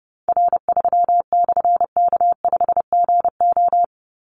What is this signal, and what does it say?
Morse code, heard as a keyed tone, about 30 wpm, 700 Hz. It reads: R3/K5GO
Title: Call Signs Repeated 3x - 30wpm